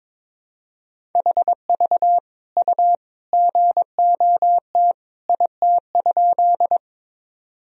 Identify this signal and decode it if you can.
Morse code, heard as a keyed tone, about 22 wpm, 700 Hz. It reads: HV U GOT IT?
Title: Sentences from Top 300 Words and Abbreviations 22wpm